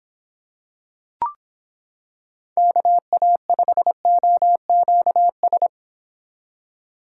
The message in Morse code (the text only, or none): KA5OQS